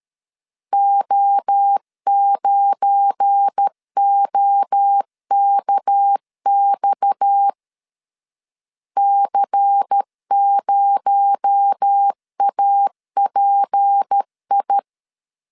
Morse code, heard as a keyed tone, about 13 wpm, 800 Hz. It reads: O9OKX C0API
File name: LEZIONI / 8 YJ45 / 8 YJ4 12wpm 2151